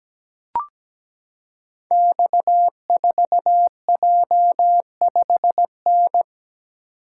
Morse code, heard as a keyed tone, about 17 wpm, 700 Hz. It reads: X4J5N
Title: Code Groups - 5 Letter:Numbers - 17wpm